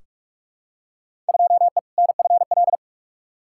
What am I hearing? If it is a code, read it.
Morse code, heard as a keyed tone, about 42 wpm, 700 Hz. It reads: 2E NFL
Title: Field Day Exchanges Encore - 45wpm